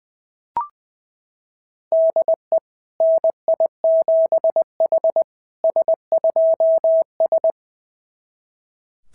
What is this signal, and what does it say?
Morse code, heard as a keyed tone, about 20 wpm, 660 Hz. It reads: DE NI7H S2S
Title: SOTA QSO Practice 20wpm